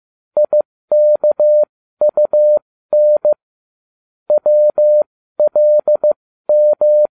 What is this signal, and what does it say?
Morse code, heard as a keyed tone, about 15 wpm, spaced as for 13 wpm, 600 Hz. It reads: IKUN WLM